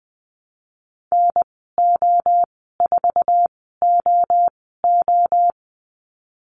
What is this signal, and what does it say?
Morse code, heard as a keyed tone, about 19 wpm, spaced as for 14 wpm, 700 Hz. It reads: NO4OO
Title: 200 Lesson 32g - Letter k - Call Signs 20wpm